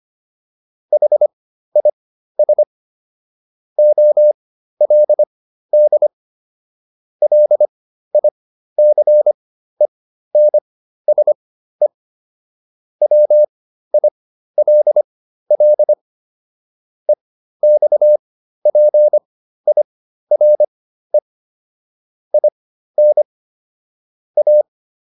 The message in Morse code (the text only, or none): HIS OLD LICENSE WILL EXPIRE IN A